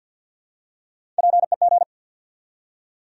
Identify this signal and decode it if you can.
Morse code, heard as a keyed tone, about 49 wpm, 700 Hz. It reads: PEG
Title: USA Common Names - Max 3 Letters - 50wpm